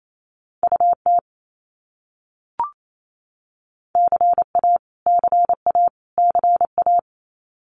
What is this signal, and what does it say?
Morse code, heard as a keyed tone, about 28 wpm, 700 Hz. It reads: UT CA CA CA
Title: US States Repeated 3x 28wpm